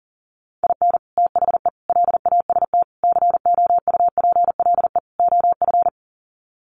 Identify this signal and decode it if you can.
Morse code, heard as a keyed tone, about 40 wpm, 700 Hz. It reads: IN THE LAST COUPLE OF
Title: Sets of 5 Words 40wpm